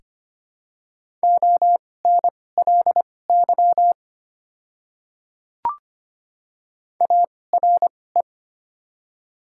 Morse code, heard as a keyed tone, about 25 wpm, spaced as for 18 wpm, 700 Hz. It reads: ONLY ARE
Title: 2x Character Spacing - Top 200 Words - 25wpm